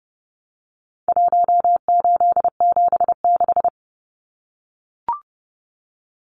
Morse code, heard as a keyed tone, about 30 wpm, 700 Hz. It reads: 1876
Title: Numbers - Contest Style - 4-Digit 30wpm